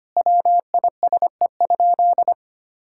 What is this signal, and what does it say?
Morse code, heard as a keyed tone, about 25 wpm, 700 Hz. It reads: WISE?
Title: Sentences from Top 1600 Words 25wpm